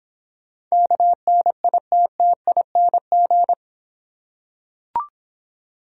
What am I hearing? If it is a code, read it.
Morse code, heard as a keyed tone, about 26 wpm, 700 Hz. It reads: KNITTING